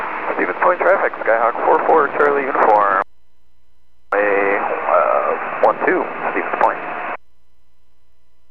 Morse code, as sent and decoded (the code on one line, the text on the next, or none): none